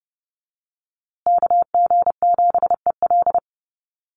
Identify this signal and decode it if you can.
Morse code, heard as a keyed tone, about 30 wpm, 700 Hz. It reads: KG7EL
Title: USA Call Signs Repeated 3x 30wpm